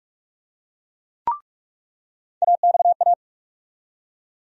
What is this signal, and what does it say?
Morse code, heard as a keyed tone, about 42 wpm, 700 Hz. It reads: AXA
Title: USA Call Sign Postfixes - 3 Letters 45wpm